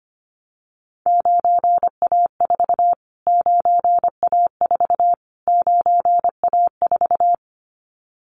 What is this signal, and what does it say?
Morse code, heard as a keyed tone, about 25 wpm, 700 Hz. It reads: 9A4 9A4 9A4
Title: Callsign Prefixes for Top 25 DXCC - Repeated 3x - 25wpm